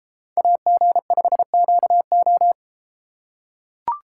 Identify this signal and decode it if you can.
Morse code, heard as a keyed tone, about 32 wpm, 700 Hz. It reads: AG5QO